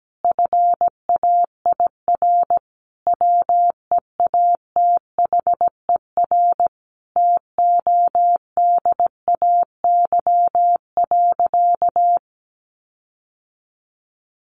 Morse code, heard as a keyed tone, about 17 wpm, 700 Hz. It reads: FAIR WEATHER TODAY.
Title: Sentences from Top 1300 Words 17wpm